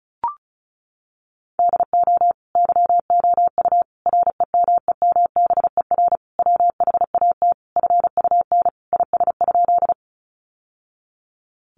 Morse code, heard as a keyed tone, about 34 wpm, 700 Hz. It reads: DO YOU REMEMBER WHAT FUN IS?
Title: Sentences from Top 700 Words 35wpm